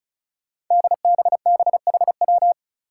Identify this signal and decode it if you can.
Morse code, heard as a keyed tone, about 36 wpm, 700 Hz. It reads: DBBHW
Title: Five Challenging Letter-Number Combinations 35wpm